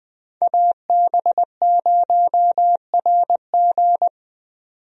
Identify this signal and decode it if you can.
Morse code, heard as a keyed tone, about 20 wpm, 700 Hz. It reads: AB0RG
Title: USA Call Signs - 5 Characters - 20wpm